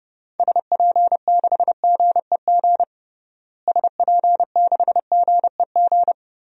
Morse code, heard as a keyed tone, about 31 wpm, 700 Hz. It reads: SP6GEG SP6GEG
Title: DXCC-100 Call Signs - 6 Characters - Repeated 3x - 30wpm